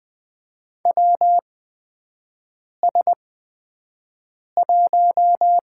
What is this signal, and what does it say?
Morse code, heard as a keyed tone, about 20 wpm, spaced as for 5 wpm, 700 Hz. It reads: WS1
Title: USA Call Signs - 4 Characters - 8x Character Spacing - 20wpm